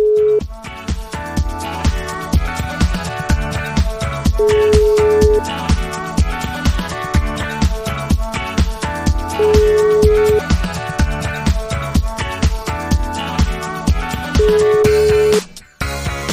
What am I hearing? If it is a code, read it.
Ringback tone (Europe, 425 Hz), with noise.